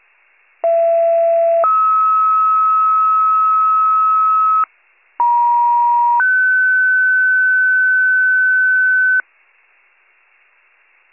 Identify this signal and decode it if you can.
Two-tone page, heard as a one-second tone, then a three-second tone, about 670 then 1290 Hz, about 950 then 1500 Hz. It is heard from 0.6 to 9.2 s.